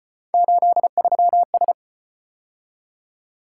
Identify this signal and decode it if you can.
Morse code, heard as a keyed tone, about 35 wpm, 700 Hz. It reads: 83S